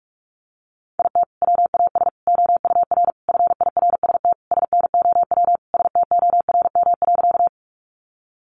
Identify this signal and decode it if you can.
Morse code, heard as a keyed tone, about 44 wpm, 700 Hz. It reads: IT WAS OUR FIRST SNOW STORM.
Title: Sentences from Top 1500 Words 45wpm